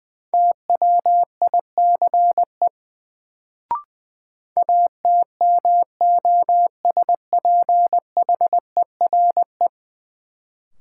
Morse code, heard as a keyed tone, about 20 wpm, 700 Hz. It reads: TWICE ATMOSPHERE
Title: Sentences from Top 1600 Words - New Word Drill - 20wpm